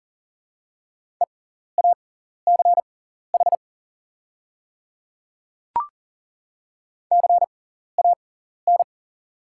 Morse code, heard as a keyed tone, about 41 wpm, spaced as for 14 wpm, 690 Hz. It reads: EACH CAN
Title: 6x Character Spacing - Top 100 Words - 40wpm